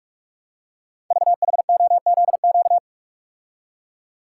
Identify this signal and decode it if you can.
Morse code, heard as a keyed tone, about 44 wpm, 700 Hz. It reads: VHOZQ
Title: Code Groups - 5 Letters - 45wpm